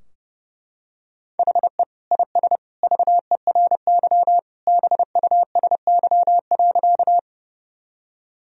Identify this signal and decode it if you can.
Morse code, heard as a keyed tone, about 30 wpm, 700 Hz. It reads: HE IS VERY BUSY.
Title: Sentences from Top 1000 Words - Review Part 3 - 30wpm